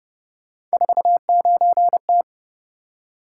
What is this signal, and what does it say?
Morse code, heard as a keyed tone, about 30 wpm, 700 Hz. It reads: 49T